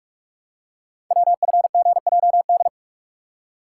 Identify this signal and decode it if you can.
Morse code, heard as a keyed tone, about 44 wpm, 700 Hz. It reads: WFGJD
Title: Code Groups - 5 Letter:Numbers - 45wpm